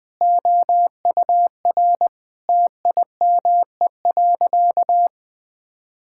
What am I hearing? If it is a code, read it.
Morse code, heard as a keyed tone, about 20 wpm, 700 Hz. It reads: OUR TIME.
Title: Sentences with 4-Word Phrases - 20wpm